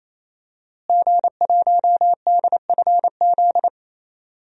Morse code, heard as a keyed tone, about 28 wpm, 700 Hz. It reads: G1DFZ
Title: DXCC-100 Call Signs - 5 Characters - Repeated 3x - 28wpm